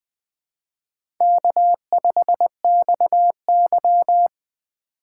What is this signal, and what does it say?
Morse code, heard as a keyed tone, about 20 wpm, 700 Hz. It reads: K5XY
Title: USA Call Signs - 4 Characters - 2x Character Spacing - 20wpm